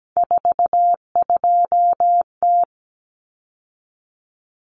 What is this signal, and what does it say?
Morse code, heard as a keyed tone, about 17 wpm, 690 Hz. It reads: 42T
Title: Common Cut Numbers - 3-Digit 17wpm